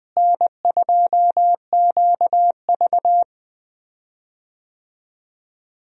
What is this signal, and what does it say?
Morse code, heard as a keyed tone, about 20 wpm, 700 Hz. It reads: N2QV
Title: USA Call Signs - 4 Characters - Repeated 2x - 20wpm